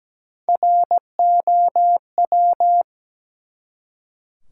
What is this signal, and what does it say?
Morse code, heard as a keyed tone, about 17 wpm, 700 Hz. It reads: ROW
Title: Three Letter Words - 17wpm